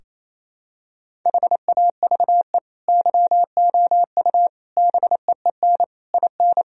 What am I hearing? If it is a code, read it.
Morse code, heard as a keyed tone, about 28 wpm, 700 Hz. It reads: HAVE YOU BEEN IN
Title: Sentences from Top 1000 Words - Review Part 2 - 28wpm